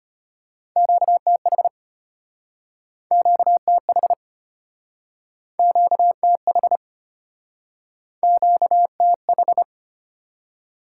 Morse code, heard as a keyed tone, about 33 wpm, 700 Hz. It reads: QTH QTH QTH QTH